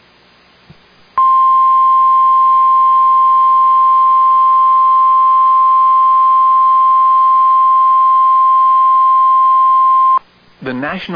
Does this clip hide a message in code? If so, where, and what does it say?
Yes: Emergency Alert System (SAME) tones (the 1050 Hz attention tone) from 1.1 to 10.2 s, with speech.